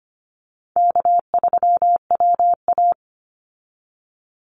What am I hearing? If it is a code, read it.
Morse code, heard as a keyed tone, about 25 wpm, 700 Hz. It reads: K3WA
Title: Easy Call Signs Repeated 3x - 25wpm